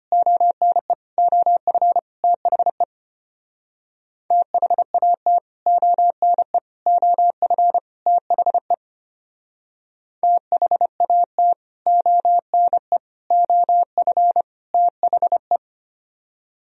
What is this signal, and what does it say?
Morse code, heard as a keyed tone, about 32 wpm, 700 Hz. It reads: ONE OF THE THAT ONE OF THE THAT ONE OF THE